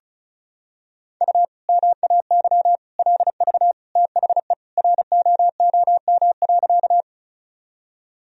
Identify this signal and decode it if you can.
Morse code, heard as a keyed tone, about 34 wpm, 700 Hz. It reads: U MAY LV THE ROOM.
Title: Sentences from Top 400 Words and Abbreviations 35wpm